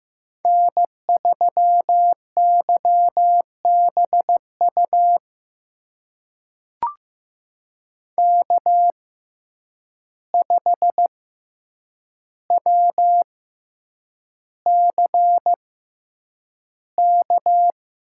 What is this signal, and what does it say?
Morse code, heard as a keyed tone, about 15 wpm, spaced as for 8 wpm, 700 Hz. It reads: N3YBU K5WCK
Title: USA Call Signs - 5 Characters - 6x Character Spacing - 15wpm